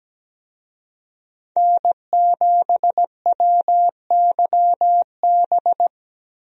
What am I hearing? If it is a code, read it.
Morse code, heard as a keyed tone, about 17 wpm, 700 Hz. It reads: N7WYB
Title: USA Call Signs - 5 Characters - Repeated 3x - 17wpm